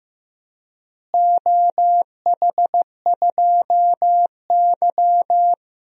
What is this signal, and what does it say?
Morse code, heard as a keyed tone, about 15 wpm, 700 Hz. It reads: OH2Y
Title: DXCC-100 Call Signs - 4 Characters - Repeated 2x - 15wpm